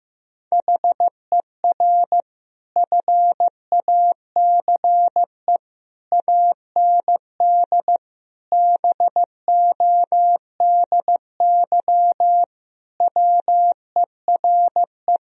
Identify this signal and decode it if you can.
Morse code, heard as a keyed tone, about 16 wpm, 700 Hz. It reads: HER FACE AND BODY WERE
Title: Sentences from Top 1000 Words - Review Part 2 - 15wpm